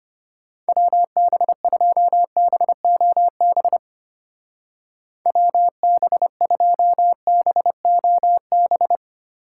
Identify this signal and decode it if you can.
Morse code, heard as a keyed tone, about 32 wpm, 700 Hz. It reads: WB2BOB WB2BOB